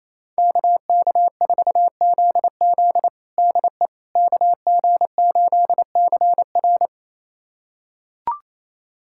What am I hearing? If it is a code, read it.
Morse code, heard as a keyed tone, about 27 wpm, 700 Hz. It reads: KK4ZZ DE KG8CR
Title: USA Callsign DE Callsign - 5 Characters - 28wpm